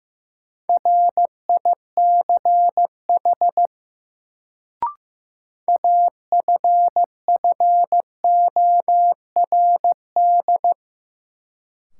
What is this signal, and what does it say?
Morse code, heard as a keyed tone, about 15 wpm, 700 Hz. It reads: RICH AFFORD
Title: Sentences from Top 1100 Words - New Word Drill Alt - 15wpm